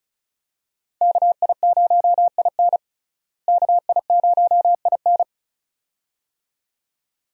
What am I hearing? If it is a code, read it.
Morse code, heard as a keyed tone, about 34 wpm, 700 Hz. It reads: KI0IN KI0IN
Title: USA Call Signs - 5 Characters - Repeated 2x - 35wpm